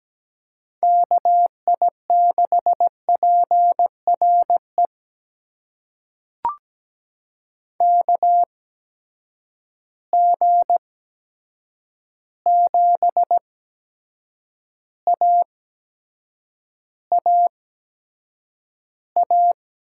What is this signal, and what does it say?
Morse code, heard as a keyed tone, about 17 wpm, spaced as for 7 wpm, 700 Hz. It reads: KI6PRE KG7AAA
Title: USA Call Signs - 6 Characters - 8x Character Spacing - 17wpm